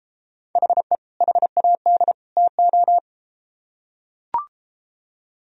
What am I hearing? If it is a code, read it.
Morse code, heard as a keyed tone, about 32 wpm, 700 Hz. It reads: HE HAD TO